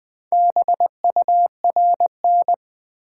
Morse code, heard as a keyed tone, about 20 wpm, 700 Hz. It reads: BURN
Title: Sentences from Top 1300 Words - New Word Drill - 20wpm